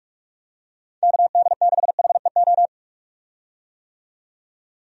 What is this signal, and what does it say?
Morse code, heard as a keyed tone, about 47 wpm, 700 Hz. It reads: KD6HEO
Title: USA Call Signs - 6 Characters - 45wpm